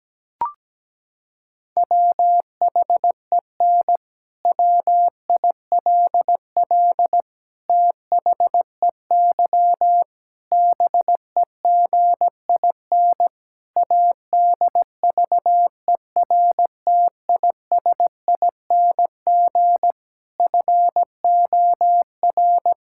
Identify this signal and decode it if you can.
Morse code, heard as a keyed tone, about 17 wpm, 700 Hz. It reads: WHEN WILL THEY BEGIN ADVERTISING FOR
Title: Sentences from Top 1000 Words - Review Part 6 - 17wpm